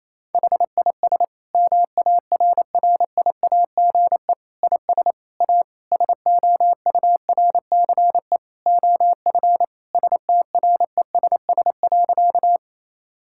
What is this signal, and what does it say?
Morse code, heard as a keyed tone, about 28 wpm, 700 Hz. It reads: HIS MARRIAGE IS A SOURCE OF STRESS.
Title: Sentences from Top 1000 Words - Review Part 6 - 28wpm